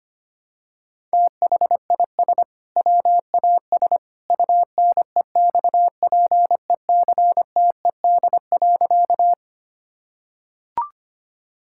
Morse code, heard as a keyed tone, about 25 wpm, 700 Hz. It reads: THIS WAS UNEXPECTED.
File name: practice / 25wpm / Sentences with 2-Word Phrases - 25wpm